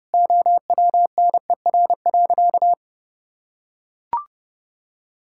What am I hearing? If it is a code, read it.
Morse code, heard as a keyed tone, about 30 wpm, 700 Hz. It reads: OWNER.
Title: Sentences from Top 1600 Words 30wpm